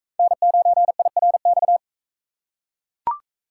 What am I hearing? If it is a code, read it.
Morse code, heard as a keyed tone, about 44 wpm, 700 Hz. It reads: N9IRX